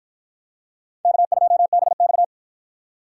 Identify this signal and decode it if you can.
Morse code, heard as a keyed tone, about 49 wpm, 700 Hz. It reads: K2BX